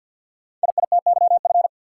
Morse code, heard as a keyed tone, about 49 wpm, 700 Hz. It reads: IITYF
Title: Code Groups - 5 Letters - 50wpm